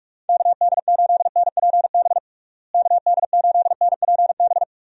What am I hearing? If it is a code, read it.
Morse code, heard as a keyed tone, about 46 wpm, 700 Hz. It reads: KD8NPB KD8NPB
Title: USA Call Signs - 6 Characters - Repeated 3x - 45wpm